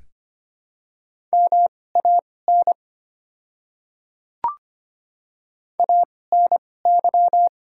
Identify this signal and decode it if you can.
Morse code, heard as a keyed tone, about 25 wpm, spaced as for 18 wpm, 700 Hz. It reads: MAN ANY
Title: 2x Character Spacing - Top 500 Words - 25wpm